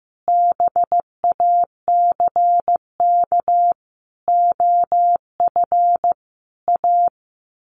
Morse code, heard as a keyed tone, about 15 wpm, 700 Hz. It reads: BACK OF A
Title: Sets of 5 Words Encore2 15wpm